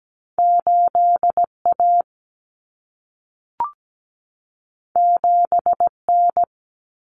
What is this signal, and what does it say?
Morse code, heard as a keyed tone, about 17 wpm, 700 Hz. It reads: 8A 7N
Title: Common Cut Numbers - 2-Digit 17wpm